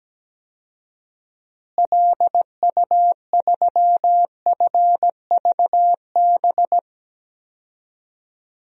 Morse code, heard as a keyed tone, about 17 wpm, 700 Hz. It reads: LU3FVB
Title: DXCC-100 Call Signs - 6 Characters - Repeated 3x - 17wpm